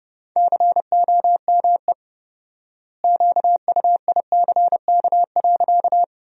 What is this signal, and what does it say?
Morse code, heard as a keyed tone, about 32 wpm, 700 Hz. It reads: COME QUICK.
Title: 4x Word-Spacing - Sentences from Top 500 Words - 30wpm